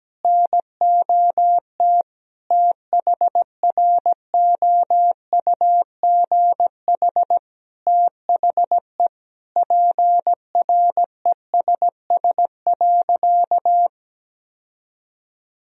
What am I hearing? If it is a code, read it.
Morse code, heard as a keyed tone, about 17 wpm, 700 Hz. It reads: NOT THROUGH THE PRESS.
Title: Sentences from Top 1000 Words - Review Part 1 - 17wpm